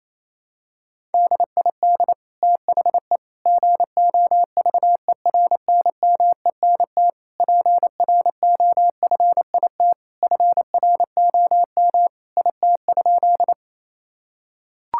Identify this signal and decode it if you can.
Morse code, heard as a keyed tone, about 27 wpm, 700 Hz. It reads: DID THE GOVERNMENT PROFIT FROM IT?
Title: Sentences from Top 1000 Words - Review Part 5 - 28wpm